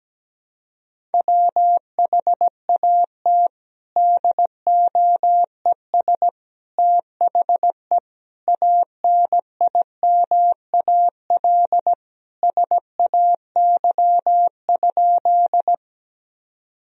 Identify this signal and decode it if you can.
Morse code, heard as a keyed tone, about 17 wpm, 700 Hz. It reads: WHAT DOES THE ANIMAL SAY?
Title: Sentences from Top 1000 Words - Review Part 1 - 17wpm